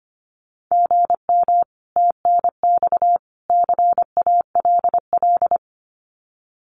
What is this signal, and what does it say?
Morse code, heard as a keyed tone, about 25 wpm, 700 Hz. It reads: GM TNX CALL
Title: Common QSO Phrases 25wpm